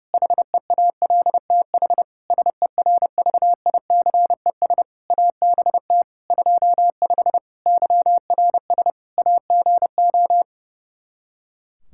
Morse code, heard as a keyed tone, about 30 wpm, 700 Hz. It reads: HEALTH SERVICES ABT 25 YRS AGO
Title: Practice QSO - Exchange 3 - More Info - 30wpm